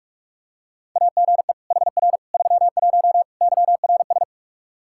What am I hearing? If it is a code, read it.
Morse code, heard as a keyed tone, about 46 wpm, 700 Hz. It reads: AGE HR 31 YRS
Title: Practice QSO - Exchange 3 - More Info - 45wpm